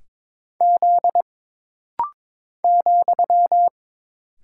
Morse code, heard as a keyed tone, about 22 wpm, 700 Hz. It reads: Z ,
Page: This rapid-fire timing is geared to push your ICR (Instant Character Recognition) skills.